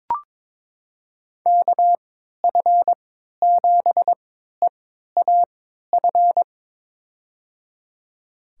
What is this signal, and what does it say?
Morse code, heard as a keyed tone, about 21 wpm, spaced as for 12 wpm, 700 Hz. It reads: KF7EAF
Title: USA Call Signs - 6 Characters - 3x Character Spacing - 22wpm